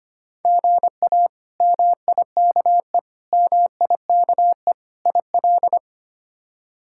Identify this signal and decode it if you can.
Morse code, heard as a keyed tone, about 25 wpm, 700 Hz. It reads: GA MIKE MIKE IL
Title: K1USN SST QSO Practice - 25wpm